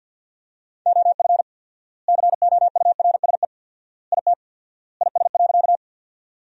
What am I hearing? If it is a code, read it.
Morse code, heard as a keyed tone, about 50 wpm, 700 Hz. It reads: OF COURSE IT IS.
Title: 4x Word-Spacing - Sentences from Top 500 Words - 50wpm